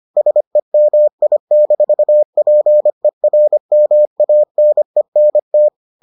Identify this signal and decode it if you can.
Morse code, heard as a keyed tone, about 25 wpm, 600 Hz. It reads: SEMI-PERMANENT